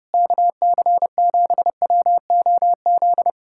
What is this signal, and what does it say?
Morse code, heard as a keyed tone, about 31 wpm, 700 Hz. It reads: KC7WOZ